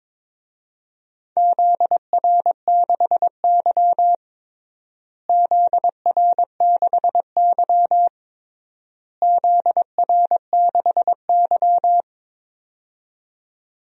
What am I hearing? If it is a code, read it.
Morse code, heard as a keyed tone, about 22 wpm, 700 Hz. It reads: ZR6Y ZR6Y ZR6Y
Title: DXCC-100 Call Signs - 4 Characters - Repeated 3x - 22wpm